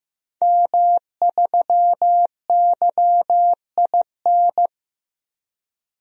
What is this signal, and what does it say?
Morse code, heard as a keyed tone, about 15 wpm, 700 Hz. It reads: M3YIN
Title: DXCC-100 Call Signs - 5 Characters - Repeated 2x - 15wpm